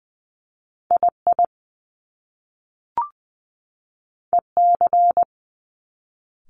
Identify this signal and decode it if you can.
Morse code, heard as a keyed tone, about 20 wpm, 700 Hz. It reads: I I E C
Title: Top Two Letter Combinations in Words 20wpm